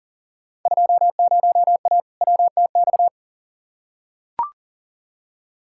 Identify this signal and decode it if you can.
Morse code, heard as a keyed tone, about 39 wpm, 700 Hz. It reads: 20A WTX
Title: Field Day Exchanges Encore - 40wpm